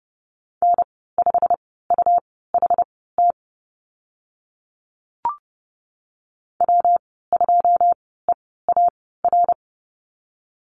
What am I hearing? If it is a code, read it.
Morse code, heard as a keyed tone, about 30 wpm, spaced as for 17 wpm, 700 Hz. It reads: N5UHT W2EAR
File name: Lesson 19g - Letter w - Call Signs 30wpm